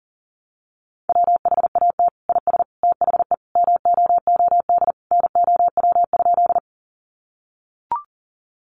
Morse code, heard as a keyed tone, about 40 wpm, 700 Hz. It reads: WHAT IS THE MOOD NOW?
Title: Sentences from Top 1100 Words 40wpm